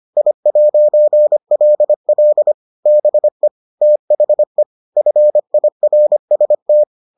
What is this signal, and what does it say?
Morse code, heard as a keyed tone, about 25 wpm, 600 Hz. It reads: I'LL BE THE FIRST